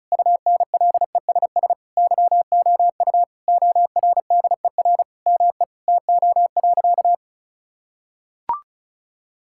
Morse code, heard as a keyed tone, about 36 wpm, 700 Hz. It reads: UNLESS YOU ORDER ME TO.
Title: Sentences from Top 1000 Words - Review Part 6 - 35wpm